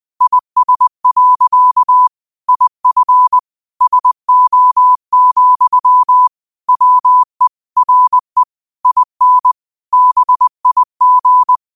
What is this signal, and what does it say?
Morse code, heard as a keyed tone, about 20 wpm, 1000 Hz. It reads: IS. IF SO, WE RE IN BIG
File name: qotdfiles / 20wpm